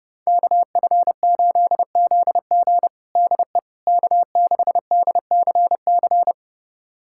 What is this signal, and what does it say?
Morse code, heard as a keyed tone, about 32 wpm, 700 Hz. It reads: KF8ZG DE K6DCC
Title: USA Callsign DE Callsign - 5 Characters - 30wpm